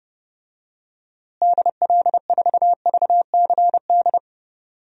Morse code, heard as a keyed tone, about 30 wpm, 700 Hz. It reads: DL4VCD
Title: DXCC-100 Call Signs - 6 Characters - Repeated 2x - 30wpm